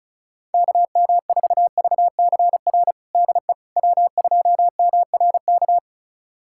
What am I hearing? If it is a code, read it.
Morse code, heard as a keyed tone, about 36 wpm, 700 Hz. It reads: KM4VCR DE W2MRK
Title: USA Callsign DE Callsign 35wpm